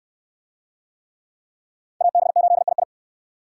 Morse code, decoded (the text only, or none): AB8II